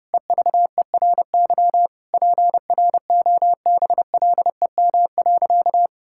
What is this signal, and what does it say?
Morse code, heard as a keyed tone, about 32 wpm, 700 Hz. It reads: EVERY PROBLEM.
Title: Sentences from Top 1000 Words - Review Part 5 - 30wpm